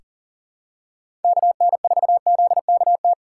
Morse code, heard as a keyed tone, about 40 wpm, 700 Hz. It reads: KN4ZKT
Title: K1USN SST QSO Practice - 40wpm